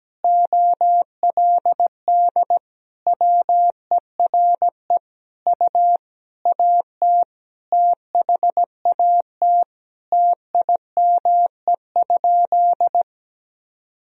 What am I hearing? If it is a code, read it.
Morse code, heard as a keyed tone, about 17 wpm, 700 Hz. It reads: OLD WERE U AT THAT TIME?
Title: Sentences from Top 200 Words and Abbreviations 17wpm